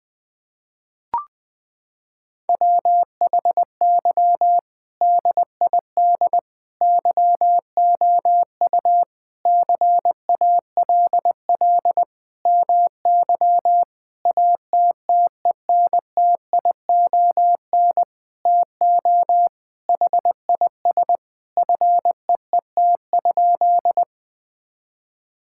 Morse code, heard as a keyed tone, about 20 wpm, 700 Hz. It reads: WHY DID YOU CALL MY ATTENTION TO HIS FEET?
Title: Sentences from Top 1000 Words - Review Part 6 - 20wpm